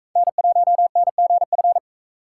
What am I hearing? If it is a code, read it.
Morse code, heard as a keyed tone, about 43 wpm, 700 Hz. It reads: N1NGF